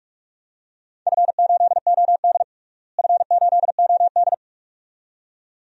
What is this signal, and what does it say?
Morse code, heard as a keyed tone, about 46 wpm, 700 Hz. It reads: F8OD F8OD
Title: DXCC-100 Call Signs - 4 Characters - Repeated 2x - 45wpm